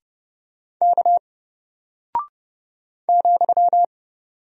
Morse code, heard as a keyed tone, about 30 wpm, 700 Hz. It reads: K ,
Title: Single Character (Letter:Number:Punctuation:Prosign) 30wpm